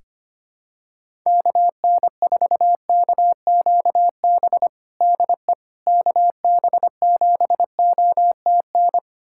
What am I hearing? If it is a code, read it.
Morse code, heard as a keyed tone, about 25 wpm, 700 Hz. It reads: KN4KQB DE KB7OTN
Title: USA Callsign DE Callsign 25wpm